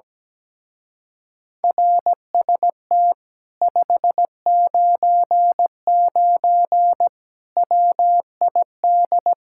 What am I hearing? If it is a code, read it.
Morse code, heard as a keyed tone, about 17 wpm, 700 Hz. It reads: RST 599 WID
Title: Speed-Racing Common QSO Phrases Encore - 17wpm